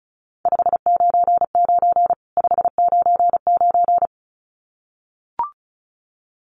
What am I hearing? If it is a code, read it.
Morse code, heard as a keyed tone, about 34 wpm, 700 Hz. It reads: 599 599
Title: Common QSO Phrases 35wpm